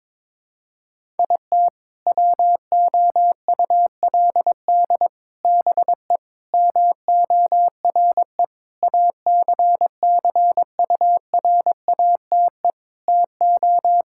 Morse code, heard as a keyed tone, about 22 wpm, 700 Hz. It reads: IT WOULD BE MORE ACCURATE TO
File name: practice / 22wpm / Sets of 8 Words - 22wpm